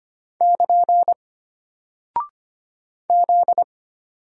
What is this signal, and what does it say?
Morse code, heard as a keyed tone, about 25 wpm, 700 Hz. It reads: <KN> Z
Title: Single Character (Letter:Number:Punctuation:Prosign) Alt 25wpm